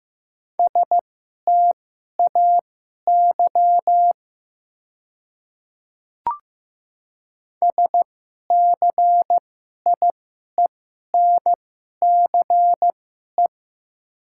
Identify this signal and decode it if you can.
Morse code, heard as a keyed tone, about 15 wpm, spaced as for 11 wpm, 700 Hz. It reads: STAY SCIENCE
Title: 2x Character Spacing - Top 500 Words - 15wpm